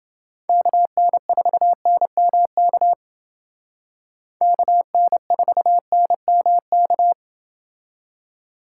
Morse code, heard as a keyed tone, about 28 wpm, 700 Hz. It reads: KN4NMK KN4NMK